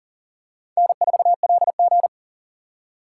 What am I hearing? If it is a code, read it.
Morse code, heard as a keyed tone, about 39 wpm, 690 Hz. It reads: N4LG
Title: USA Call Signs - 4 Characters - 2x Character Spacing - 40wpm